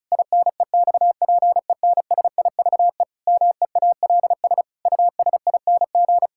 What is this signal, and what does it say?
Morse code, heard as a keyed tone, about 34 wpm, 700 Hz. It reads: INEXPENSIVE MEALS USING
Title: Pangram Challenge v2 - 35wpm